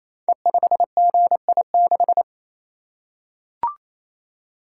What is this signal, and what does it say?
Morse code, heard as a keyed tone, about 28 wpm, 700 Hz. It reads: E5GI6
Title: Code Groups - 5 Letter:Numbers - 28wpm